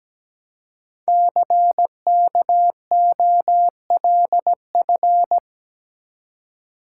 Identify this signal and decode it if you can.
Morse code, heard as a keyed tone, about 17 wpm, 700 Hz. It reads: CKOLF
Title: Code Groups - 5 Letters - 17wpm